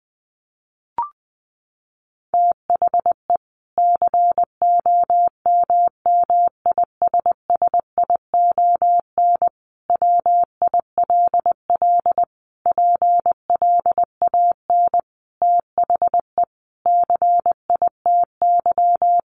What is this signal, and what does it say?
Morse code, heard as a keyed tone, about 20 wpm, 700 Hz. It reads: THE COMMISSION WILL PLAN THE CITY
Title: Sentences from Top 1400 Words 20wpm